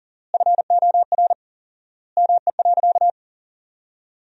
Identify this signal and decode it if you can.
Morse code, heard as a keyed tone, about 37 wpm, 700 Hz. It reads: FOR ME.
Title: 4x Word-Spacing - Sentences from Top 400 Words - 40wpm